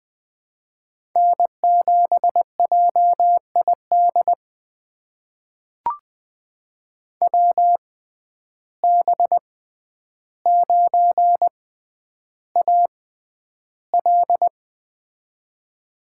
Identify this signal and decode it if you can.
Morse code, heard as a keyed tone, about 20 wpm, spaced as for 10 wpm, 700 Hz. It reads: N7JID WB9AL
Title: USA Call Signs - 6x Character Spacing - 20wpm